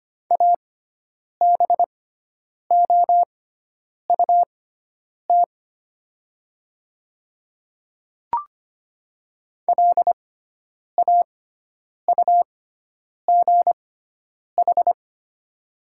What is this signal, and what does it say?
Morse code, heard as a keyed tone, about 25 wpm, spaced as for 9 wpm, 700 Hz. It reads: ABOUT LAUGH
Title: 6x Character Spacing - Top 500 Words - 25wpm